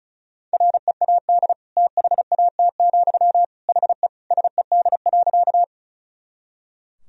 Morse code, heard as a keyed tone, about 34 wpm, 700 Hz. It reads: READ THAT, HE SED.
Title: Sentences from Top 200 Words and Abbreviations 35wpm